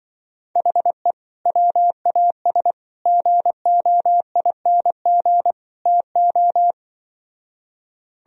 Morse code, heard as a keyed tone, about 24 wpm, 700 Hz. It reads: HE WAS GOING TO
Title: Speed-Racing - Sets of 4 Words 20wpm